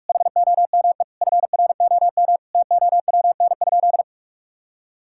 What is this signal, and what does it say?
Morse code, heard as a keyed tone, about 42 wpm, 700 Hz. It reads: HOME FROM TOWN?
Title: Sentences from Top 1000 Words - Review Part 5 - 45wpm